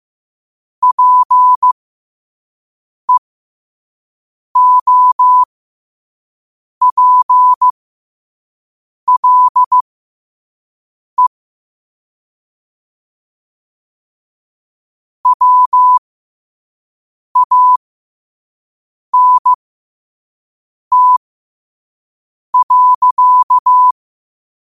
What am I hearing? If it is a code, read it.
Morse code, heard as a keyed tone, about 15 wpm, spaced as for 5 wpm, 1000 Hz. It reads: PEOPLE WANT.
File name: qotdfiles / 5wpm